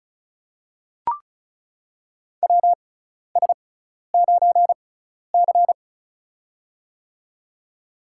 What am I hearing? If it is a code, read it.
Morse code, heard as a keyed tone, about 36 wpm, spaced as for 12 wpm, 700 Hz. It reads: WS9C
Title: USA Call Signs - 4 Characters - 6x Character Spacing - 35wpm